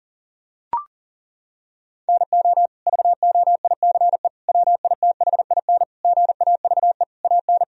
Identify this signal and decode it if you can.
Morse code, heard as a keyed tone, about 37 wpm, 700 Hz. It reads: NO VOICE WITHIN GAVE AN
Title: Sentences from Top 1000 Words - Review Part 4 - 40wpm